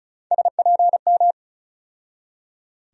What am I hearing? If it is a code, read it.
Morse code, heard as a keyed tone, about 36 wpm, 700 Hz. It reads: SPM
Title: USA Call Sign Postfixes - 3 Letters 35wpm